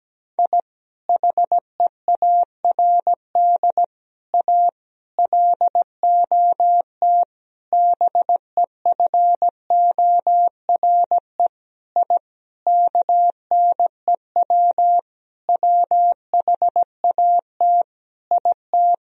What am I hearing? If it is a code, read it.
Morse code, heard as a keyed tone, about 17 wpm, 700 Hz. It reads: I HEARD A LOT BEFORE I KNEW WHAT IT
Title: Sentences from Top 1000 Words - Review Part 3 - 17wpm